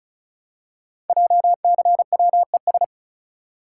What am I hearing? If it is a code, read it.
Morse code, heard as a keyed tone, about 36 wpm, 700 Hz. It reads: JCWES
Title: Code Groups - 5 Letters - 35wpm